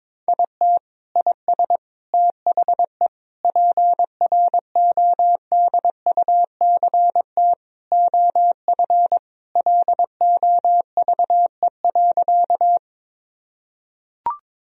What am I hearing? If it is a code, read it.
Morse code, heard as a keyed tone, about 22 wpm, 700 Hz. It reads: IT IS THE PRODUCT OF LOVE.
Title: Sentences from Top 1000 Words - Review Part 2 - 22wpm